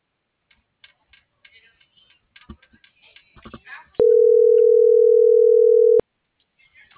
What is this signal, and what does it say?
Ringback tone (North America), 440 + 480 Hz.